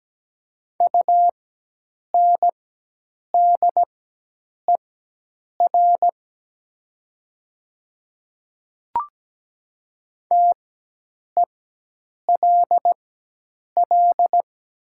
Morse code, heard as a keyed tone, about 17 wpm, spaced as for 8 wpm, 700 Hz. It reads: UNDER TELL
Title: 4x Character Spacing - Top 200 Words - 17wpm